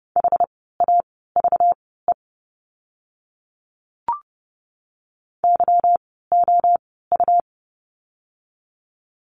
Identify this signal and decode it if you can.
Morse code, heard as a keyed tone, about 30 wpm, spaced as for 17 wpm, 700 Hz. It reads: HAVE YOU
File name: course / 30:10wpm / 237 Lesson 38f - Letter q - Words 30wpm